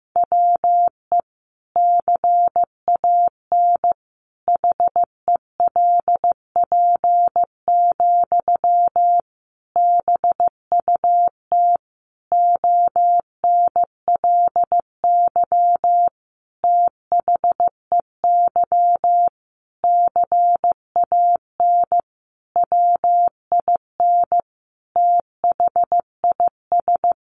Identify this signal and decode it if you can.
Morse code, heard as a keyed tone, about 15 wpm, 700 Hz. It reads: WE CAN HELP, BUT ONLY THEY CAN WIN THIS
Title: Sentences from Top 1500 Words 15wpm